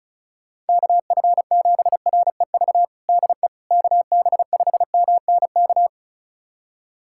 Morse code, heard as a keyed tone, about 36 wpm, 700 Hz. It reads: KF7REV DE KB5MNK
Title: USA Callsign DE Callsign - 6 Characters - 35wpm